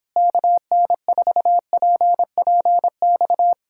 Morse code, heard as a keyed tone, about 26 wpm, 700 Hz. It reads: KN4PPX
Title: Speed-Racing Common QSO Phrases - 22wpm